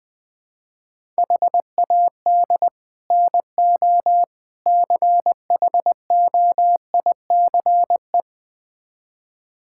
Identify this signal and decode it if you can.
Morse code, heard as a keyed tone, about 21 wpm, 700 Hz. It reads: HAD NO CHOICE
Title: Build A Phrase - 5-Word Phrases - 20wpm